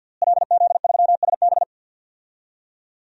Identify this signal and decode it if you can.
Morse code, heard as a keyed tone, about 50 wpm, 700 Hz. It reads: LZ3SB
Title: DXCC-100 Call Signs - 5 Characters - Repeated 3x - 50wpm